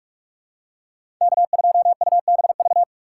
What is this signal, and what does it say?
Morse code, heard as a keyed tone, about 44 wpm, 700 Hz. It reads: K2UBV
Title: USA Call Signs - 5 Characters - 6x Character Spacing - 45wpm